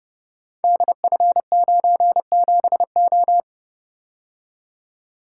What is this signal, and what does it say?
Morse code, heard as a keyed tone, about 30 wpm, 700 Hz. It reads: DF97O
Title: Code Groups - 5 Letter:Numbers - 30wpm